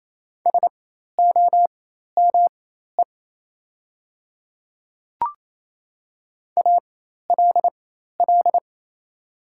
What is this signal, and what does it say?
Morse code, heard as a keyed tone, about 28 wpm, spaced as for 13 wpm, 700 Hz. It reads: SOME ALL
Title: 4x Character Spacing - Top 100 Words - 28wpm